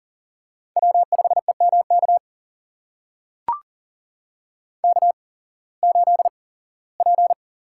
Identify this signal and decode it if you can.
Morse code, heard as a keyed tone, about 42 wpm, spaced as for 21 wpm, 700 Hz. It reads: W5EMK K8P
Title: USA Call Signs - 5 Characters - 8x Character Spacing - 40wpm